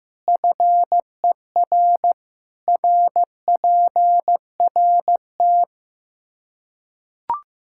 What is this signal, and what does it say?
Morse code, heard as a keyed tone, about 15 wpm, 700 Hz. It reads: FER RPRT
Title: Practice QSO - Exchange 2 - Info - 15wpm